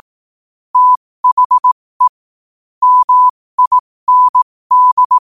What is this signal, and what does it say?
Morse code, heard as a keyed tone, about 18 wpm, spaced as for 15 wpm, 1000 Hz. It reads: THE MIND
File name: qotdfiles / 15wpm